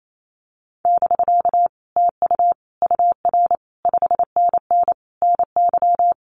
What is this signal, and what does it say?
Morse code, heard as a keyed tone, about 28 wpm, 700 Hz. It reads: <BK> TU UR 5NN NY
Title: Common QSO Phrases Encore 28wpm